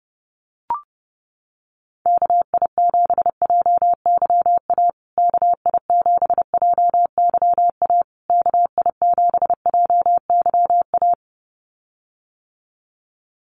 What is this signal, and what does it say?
Morse code, heard as a keyed tone, about 28 wpm, 700 Hz. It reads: KI7JYA KI7JYA KI7JYA
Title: USA Call Signs Repeated 3x 30wpm